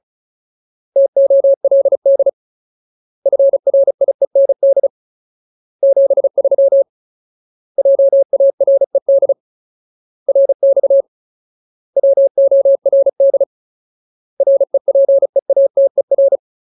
Morse code, heard as a keyed tone, about 34 wpm, 550 Hz. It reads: TOLD FRIEND 73 JARED RX WORD REPEATER
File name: Ultimate Word List / 35WPM Ultimate